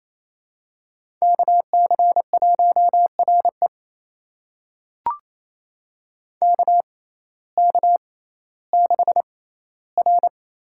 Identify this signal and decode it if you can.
Morse code, heard as a keyed tone, about 28 wpm, spaced as for 15 wpm, 700 Hz. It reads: KC1RE KK6R
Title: USA Call Signs - 6x Character Spacing - 28wpm